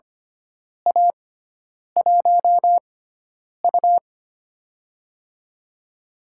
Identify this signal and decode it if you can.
Morse code, heard as a keyed tone, about 25 wpm, spaced as for 9 wpm, 700 Hz. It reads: A1U
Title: USA Call Signs - 6x Character Spacing - 25wpm